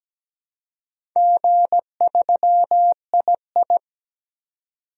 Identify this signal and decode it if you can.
Morse code, heard as a keyed tone, about 17 wpm, 700 Hz. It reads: G3II
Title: DXCC-100 Call Signs - 4 Characters - 17wpm